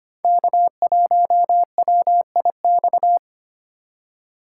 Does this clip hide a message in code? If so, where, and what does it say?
Yes: Morse code from 0.2 to 3.2 s, heard as a keyed tone, about 32 wpm, 700 Hz. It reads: K1WIX